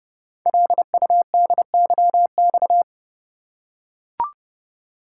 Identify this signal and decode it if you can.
Morse code, heard as a keyed tone, about 30 wpm, 700 Hz. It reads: LUDYX
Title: Code Groups - 5 Letters - 30wpm